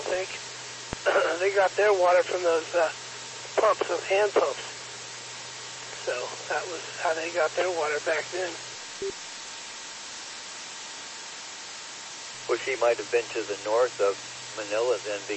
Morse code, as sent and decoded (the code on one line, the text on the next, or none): none